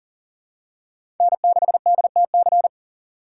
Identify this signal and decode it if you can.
Morse code, heard as a keyed tone, about 40 wpm, 700 Hz. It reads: N6DTC
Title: USA Call Signs - 3x Character Spacing - 40wpm